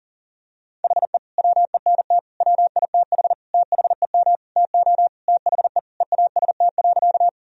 Morse code, heard as a keyed tone, about 37 wpm, 700 Hz. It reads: HE WENT WITH THEM TO THE EAST.
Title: Sentences from Top 1000 Words - Review Part 3 - 40wpm